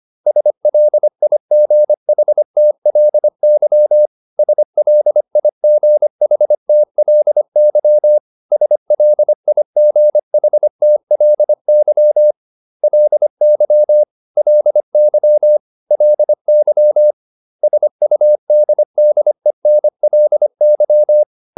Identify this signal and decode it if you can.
Morse code, heard as a keyed tone, about 25 wpm, 600 Hz. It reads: SLIGHTLY SLIGHTLY SLIGHTLY LY LY LY SUDDENLY
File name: LY8 25